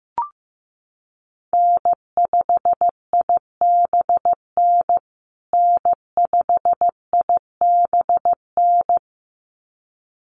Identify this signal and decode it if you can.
Morse code, heard as a keyed tone, about 15 wpm, 700 Hz. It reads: N5IBN N5IBN
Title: USA Call Signs Repeated 2x 15wpm